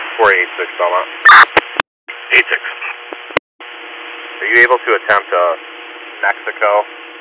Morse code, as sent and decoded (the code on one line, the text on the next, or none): none